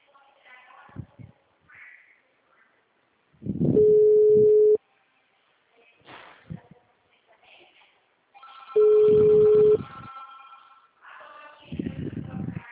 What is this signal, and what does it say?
Ringback tone (Europe, 425 Hz), with speech.